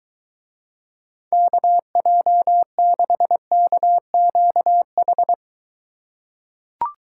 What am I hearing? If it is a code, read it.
Morse code, heard as a keyed tone, about 20 wpm, 700 Hz. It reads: KJ6KQH